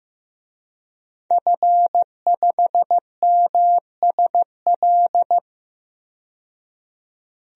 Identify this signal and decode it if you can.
Morse code, heard as a keyed tone, about 15 wpm, 700 Hz. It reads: F5MSL
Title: DXCC-100 Call Signs - 5 Characters - 15wpm